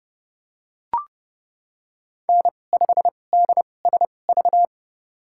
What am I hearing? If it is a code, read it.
Morse code, heard as a keyed tone, about 29 wpm, spaced as for 21 wpm, 700 Hz. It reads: N5DSV
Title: USA Call Signs - 2x Character Spacing - 30wpm